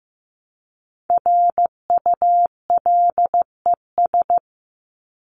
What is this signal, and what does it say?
Morse code, heard as a keyed tone, about 15 wpm, 700 Hz. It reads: RULES
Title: ES:S Suffix - Top 100 Words with Base Words 15wpm